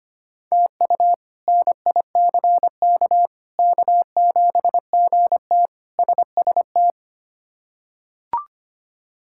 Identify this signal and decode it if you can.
Morse code, heard as a keyed tone, about 25 wpm, 700 Hz. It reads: TU NICK K7GT SST
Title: K1USN SST QSO Practice - 25wpm